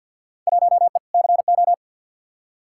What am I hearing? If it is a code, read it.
Morse code, heard as a keyed tone, about 46 wpm, 700 Hz. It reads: 1E CO
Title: Field Day Exchanges - 50wpm